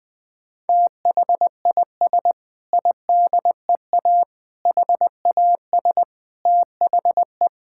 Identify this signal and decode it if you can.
Morse code, heard as a keyed tone, about 20 wpm, 700 Hz. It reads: THIS IDEA HAS THE
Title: Sentences with 6-Word Phrases - 20wpm